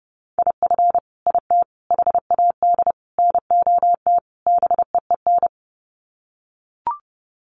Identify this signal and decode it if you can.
Morse code, heard as a keyed tone, about 30 wpm, 700 Hz. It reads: IF IT HAD NOT BEEN
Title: Sets of 5 Words Encore 30wpm